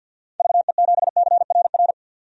Morse code, heard as a keyed tone, about 52 wpm, 700 Hz. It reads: VE7CRR
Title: DXCC-100 Call Signs - 6 Characters - 50wpm